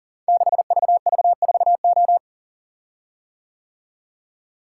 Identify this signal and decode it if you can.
Morse code, heard as a keyed tone, about 42 wpm, 700 Hz. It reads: 6VV4O
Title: Code Groups - 5 Letter:Numbers - 40wpm